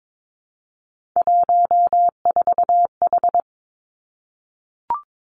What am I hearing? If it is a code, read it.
Morse code, heard as a keyed tone, about 22 wpm, 700 Hz. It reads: 14H
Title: Three Challenging Letter-Number Combinations 22wpm